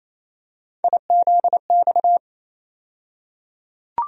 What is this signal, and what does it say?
Morse code, heard as a keyed tone, about 28 wpm, 700 Hz. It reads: IZX
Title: USA Call Sign Postfixes - 3 Letters 28wpm